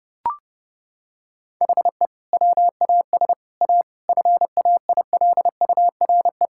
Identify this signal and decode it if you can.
Morse code, heard as a keyed tone, about 32 wpm, 700 Hz. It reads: HE WAS A FAILURE
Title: Sentences from Top 1000 Words - Review Part 6 - 30wpm